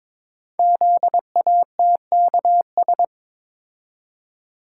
Morse code, heard as a keyed tone, about 23 wpm, 700 Hz. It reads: ZATKS
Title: Code Groups - 5 Letters - 22wpm